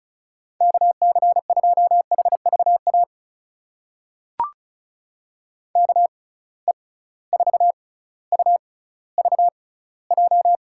Morse code, heard as a keyed tone, about 36 wpm, spaced as for 18 wpm, 700 Hz. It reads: KC2HVA KE4UVJ
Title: USA Call Signs - 6 Characters - 6x Character Spacing - 35wpm